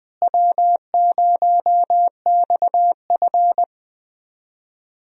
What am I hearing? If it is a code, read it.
Morse code, heard as a keyed tone, about 20 wpm, 700 Hz. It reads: W0XF
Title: USA Call Signs - 4 Characters - Repeated 3x - 20wpm